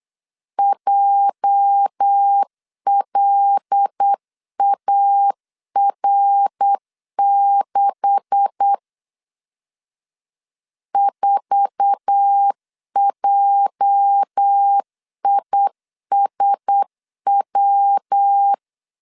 Morse code, heard as a keyed tone, about 8 wpm, 800 Hz. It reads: JLAR6 4JISW
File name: LEZIONI / 9 286 8wpm 3425